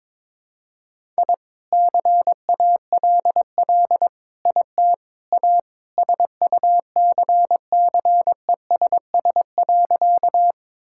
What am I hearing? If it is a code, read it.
Morse code, heard as a keyed tone, about 22 wpm, 700 Hz. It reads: I CALL IT A SUCCESS.
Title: Sentences from Top 1000 Words - Review Part 6 - 22wpm